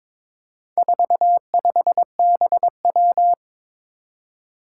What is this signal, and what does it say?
Morse code, heard as a keyed tone, about 23 wpm, 700 Hz. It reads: 45BW
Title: Four Challenging Letter-Number Combinations 22wpm